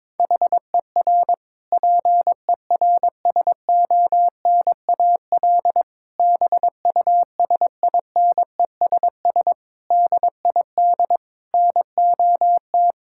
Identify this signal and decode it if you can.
Morse code, heard as a keyed tone, about 22 wpm, 700 Hz. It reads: HER PERSONAL BUSINESS DID NOT
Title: Sentences from Top 1000 Words - Review Part 6 - 22wpm